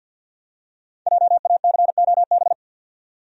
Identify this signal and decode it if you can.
Morse code, heard as a keyed tone, about 46 wpm, 700 Hz. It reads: JACOB
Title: USA Common Names - Max 5 Letters - 50wpm